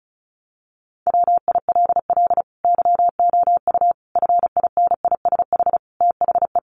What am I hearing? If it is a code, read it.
Morse code, heard as a keyed tone, about 34 wpm, 700 Hz. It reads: WILL YOU FINISH THE
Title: Sentences from Top 1300 Words 35wpm